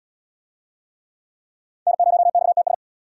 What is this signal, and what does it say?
Morse code, heard as a keyed tone, about 77 wpm, 700 Hz. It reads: N0XTS